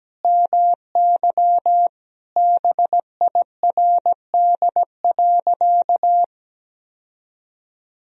Morse code, heard as a keyed tone, about 17 wpm, 700 Hz. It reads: MY BIRD.
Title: Sentences from Top 1000 Words - Review Part 2 - 17wpm